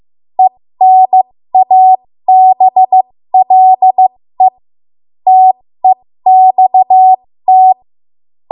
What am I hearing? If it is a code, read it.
Morse code, heard as a keyed tone, about 15 wpm, spaced as for 13 wpm, 750 Hz. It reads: ENABLE TEXT
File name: Morse / 13WPM